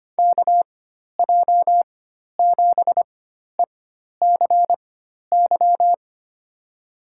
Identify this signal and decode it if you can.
Morse code, heard as a keyed tone, about 25 wpm, spaced as for 12 wpm, 700 Hz. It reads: KJ7ECY
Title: USA Call Signs - 6 Characters - 4x Character Spacing - 25wpm